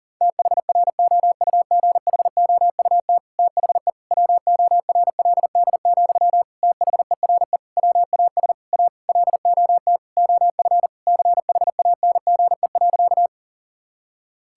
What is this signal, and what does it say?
Morse code, heard as a keyed tone, about 40 wpm, 700 Hz. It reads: THROUGHOUT THE WORLD, THERE WAS A LOT OF CHANGE.
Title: Sentences from Top 1000 Words - Review Part 5 - 40wpm